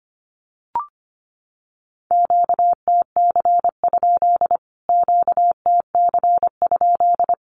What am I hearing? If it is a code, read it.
Morse code, heard as a keyed tone, about 25 wpm, 700 Hz. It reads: QTC? QTC?
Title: Q-Codes Repeated 3x 25wpm